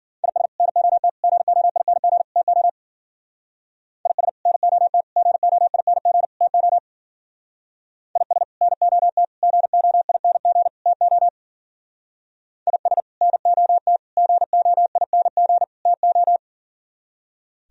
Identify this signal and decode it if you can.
Morse code, heard as a keyed tone, about 47 wpm, 700 Hz. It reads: IS NOT GOING TO IS NOT GOING TO IS NOT GOING TO IS NOT GOING TO
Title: Speed-Racing - Sets of 4 Words 40wpm